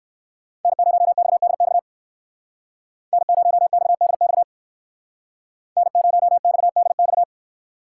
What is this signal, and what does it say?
Morse code, heard as a keyed tone, about 59 wpm, 700 Hz. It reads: N0XDX N0XDX N0XDX